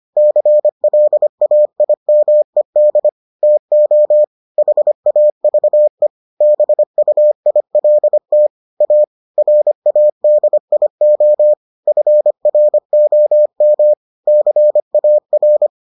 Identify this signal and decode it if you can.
Morse code, heard as a keyed tone, about 25 wpm, 600 Hz. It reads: CLAIMED TO HAVE BUILT A RADIO FROM CAR